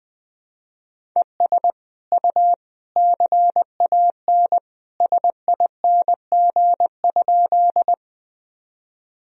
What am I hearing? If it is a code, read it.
Morse code, heard as a keyed tone, about 20 wpm, 700 Hz. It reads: ES U CAN SING?
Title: Sentences from Top 400 Words and Abbreviations 20wpm